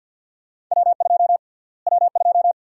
Morse code, heard as a keyed tone, about 52 wpm, 700 Hz. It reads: W2 W2
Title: USA Call Sign Prefixes - Repeated 2x - 50wpm